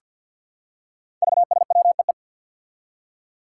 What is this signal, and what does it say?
Morse code, heard as a keyed tone, about 51 wpm, 700 Hz. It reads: VSWEE
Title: Five Challenging Letter-Number Combinations 50wpm